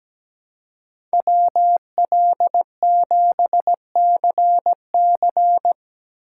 Morse code, heard as a keyed tone, about 17 wpm, 700 Hz. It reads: WL7CC
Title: USA Call Signs - 6x Character Spacing - 17wpm